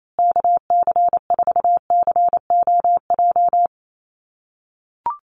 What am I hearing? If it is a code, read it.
Morse code, heard as a keyed tone, about 28 wpm, 700 Hz. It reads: KC4COJ
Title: USA Call Signs Repeated 2x 28wpm